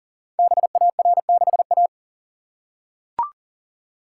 Morse code, heard as a keyed tone, about 38 wpm, 690 Hz. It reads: BAR6A
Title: Code Groups - 5 Letter:Numbers - 40wpm